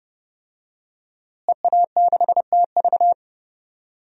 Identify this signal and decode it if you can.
Morse code, heard as a keyed tone, about 32 wpm, 700 Hz. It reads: EA6TV
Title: DXCC-100 Call Signs - 5 Characters - Repeated 2x - 30wpm